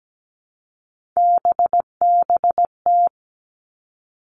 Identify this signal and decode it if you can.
Morse code, heard as a keyed tone, about 17 wpm, 700 Hz. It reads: BBT
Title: Three Challenging Letter-Number Combinations 17wpm